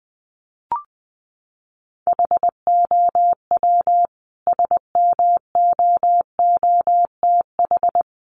Morse code, heard as a keyed tone, about 20 wpm, 700 Hz. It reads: HOW SMOOTH
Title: Sentences from Top 1400 Words 20wpm